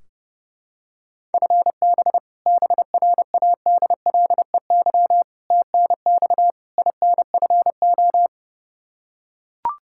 Morse code, heard as a keyed tone, about 30 wpm, 700 Hz. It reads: FB BRADLEY TNX INFO
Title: Practice QSO - Exchange 3 - More Info - 30wpm